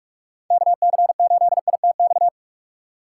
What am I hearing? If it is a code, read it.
Morse code, heard as a keyed tone, about 46 wpm, 700 Hz. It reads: KC8ITX
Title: USA Call Signs - 6 Characters - 3x Character Spacing - 45wpm